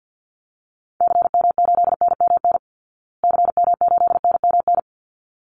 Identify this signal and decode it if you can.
Morse code, heard as a keyed tone, about 48 wpm, 700 Hz. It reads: CM8NMN CM8NMN
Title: Call Signs Repeated 3x - 50wpm